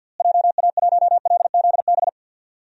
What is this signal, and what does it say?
Morse code, heard as a keyed tone, about 47 wpm, 700 Hz. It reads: JA1LZB
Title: DXCC-100 Call Signs - 6 Characters - Repeated 2x - 50wpm